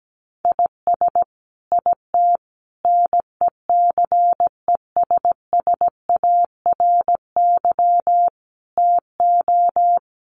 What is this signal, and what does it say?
Morse code, heard as a keyed tone, about 17 wpm, 700 Hz. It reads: IS IT NECESSARY TO
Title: Sentences from Top 1500 Words 17wpm